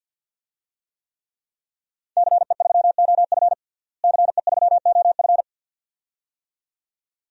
Morse code, heard as a keyed tone, about 53 wpm, 700 Hz. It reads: CE3OF CE3OF
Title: DXCC-100 Call Signs - 5 Characters - Repeated 2x - 50wpm